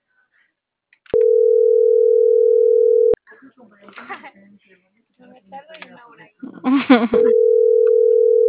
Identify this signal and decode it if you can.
Ringback tone (North America), 440 + 480 Hz, with speech.
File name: CENTRAL MABE